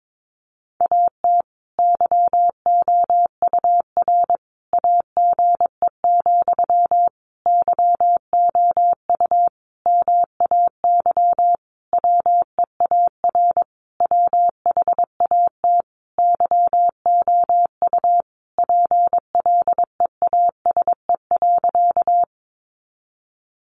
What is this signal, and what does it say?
Morse code, heard as a keyed tone, about 22 wpm, 700 Hz. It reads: AT YOUR AGE, YOU MAY WEAR WHAT YOU PLEASE.
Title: Sentences from Top 1500 Words 22wpm